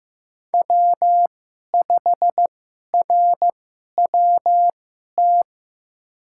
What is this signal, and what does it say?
Morse code, heard as a keyed tone, about 15 wpm, spaced as for 11 wpm, 700 Hz. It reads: W5RWT
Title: USA Call Signs - 5 Characters - 2x Character Spacing - 15wpm